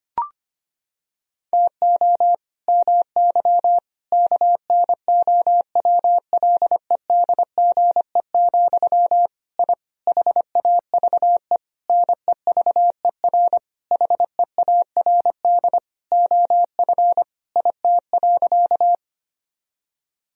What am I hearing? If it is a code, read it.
Morse code, heard as a keyed tone, about 25 wpm, 700 Hz. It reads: TO MY KNOWLEDGE, I HAVE NEVER HEARD OF IT.
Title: Sentences from Top 1000 Words - Review Part 3 - 25wpm